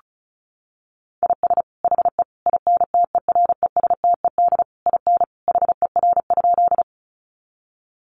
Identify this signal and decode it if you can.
Morse code, heard as a keyed tone, about 34 wpm, 700 Hz. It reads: IS HE INTERESTED IN HER?
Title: Sentences from Top 900 Words 35wpm